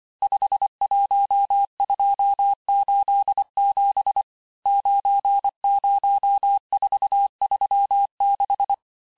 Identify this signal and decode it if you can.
Morse code, heard as a keyed tone, about 25 wpm, 800 Hz. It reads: 51287 90436
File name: texte-chiffre-vitesse-23mpm